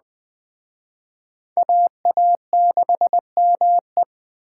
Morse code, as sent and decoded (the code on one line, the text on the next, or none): .- .- -.... -- .
AA6ME